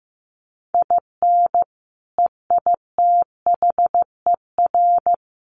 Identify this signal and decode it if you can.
Morse code, heard as a keyed tone, about 15 wpm, 700 Hz. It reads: IN EITHER
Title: Sentences from Top 1300 Words 15wpm